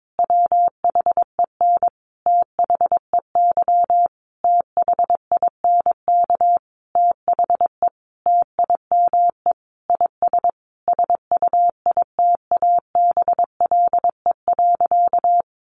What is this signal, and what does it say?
Morse code, heard as a keyed tone, about 22 wpm, 700 Hz. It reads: WHEN THEY THINK THE TIME IS SUITABLE.
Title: Sentences from Top 1400 Words 22wpm